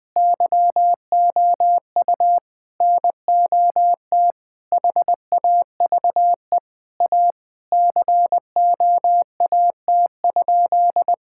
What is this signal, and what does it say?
Morse code, heard as a keyed tone, about 20 wpm, 700 Hz. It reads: YOU NOT HAVE A COAT?
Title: Sentences from Top 1600 Words 20wpm